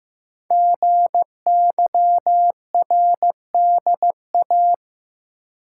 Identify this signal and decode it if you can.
Morse code, heard as a keyed tone, about 15 wpm, 700 Hz. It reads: GYRDA
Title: Code Groups - 5 Letters - 15wpm